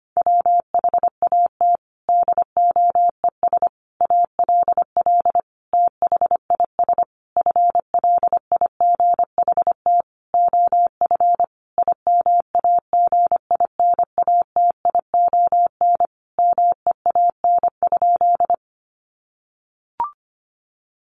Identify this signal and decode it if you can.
Morse code, heard as a keyed tone, about 25 wpm, 700 Hz. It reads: WHAT DOES ALL THIS FLIGHT OF IMAGINATION MEAN?
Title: Sentences from Top 1300 Words 25wpm